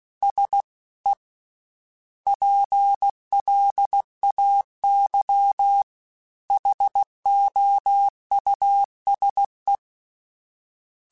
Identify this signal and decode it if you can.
Morse code, heard as a keyed tone, about 16 wpm, 800 Hz. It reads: S E PLAY HOUSE
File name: General / General-mp3s / Houses @15